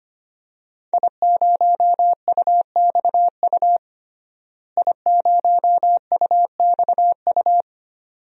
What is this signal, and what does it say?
Morse code, heard as a keyed tone, about 25 wpm, 700 Hz. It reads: I0UXU I0UXU
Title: DXCC-100 Call Signs - 5 Characters - Repeated 2x - 25wpm